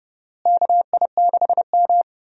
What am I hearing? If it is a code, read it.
Morse code, heard as a keyed tone, about 31 wpm, 700 Hz. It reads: KI6M
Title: USA Call Signs - 4 Characters - Repeated 2x - 30wpm